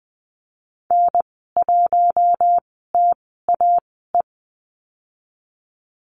Morse code, heard as a keyed tone, about 20 wpm, spaced as for 15 wpm, 700 Hz. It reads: N1TAE
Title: 062 Lesson 11g - Letter h - Call Signs 20wpm